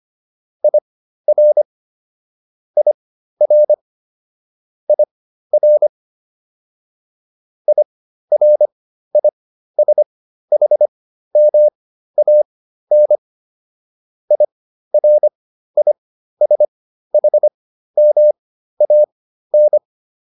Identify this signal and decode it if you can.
Morse code, heard as a keyed tone, about 25 wpm, spaced as for 13 wpm, 600 Hz. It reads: IR IR IR IRISHMAN IRISHMAN